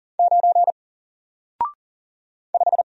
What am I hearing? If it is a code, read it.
Morse code, heard as a keyed tone, about 40 wpm, 700 Hz. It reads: 9 5
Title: Single Character (Letter:Number:Punctuation:Prosign) Alt 40wpm